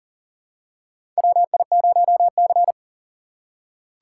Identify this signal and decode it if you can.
Morse code, heard as a keyed tone, about 40 wpm, 700 Hz. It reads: WI0C
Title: USA Call Signs - 4 Characters - 3x Character Spacing - 40wpm